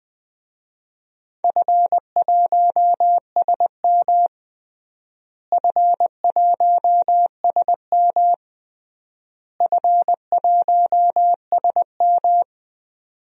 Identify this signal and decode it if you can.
Morse code, heard as a keyed tone, about 21 wpm, 700 Hz. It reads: F1SM F1SM F1SM
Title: DXCC-100 Call Signs - 4 Characters - Repeated 3x - 20wpm